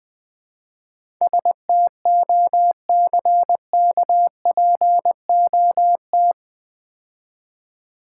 Morse code, heard as a keyed tone, about 22 wpm, 700 Hz. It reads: STOCKPOT
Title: Difficult Words - Speed-Racing with Hints - 20wpm